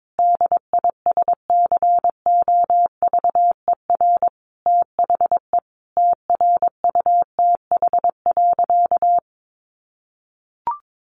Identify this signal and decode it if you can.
Morse code, heard as a keyed tone, about 22 wpm, 700 Hz. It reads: DISCOVER THE TRUTH.
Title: Sentences from Top 1100 Words 22wpm